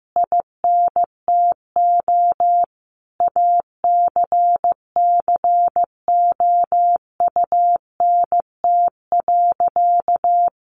Morse code, heard as a keyed tone, about 15 wpm, 700 Hz. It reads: INTO ACCOUNT.
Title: Sentences from Top 1300 Words 15wpm